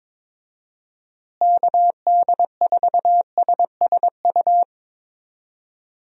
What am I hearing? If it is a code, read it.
Morse code, heard as a keyed tone, about 22 wpm, 700 Hz. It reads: KD4SSU
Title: USA Call Signs - 6 Characters - Repeated 3x - 22wpm